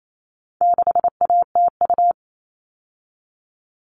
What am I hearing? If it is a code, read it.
Morse code, heard as a keyed tone, about 28 wpm, 700 Hz. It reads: 6ATU
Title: All Cut Numbers - 4-Digit 28wpm